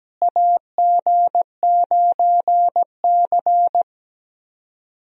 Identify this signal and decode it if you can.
Morse code, heard as a keyed tone, about 17 wpm, 700 Hz. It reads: AG9C
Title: USA Call Signs - 4 Characters - Repeated 2x - 17wpm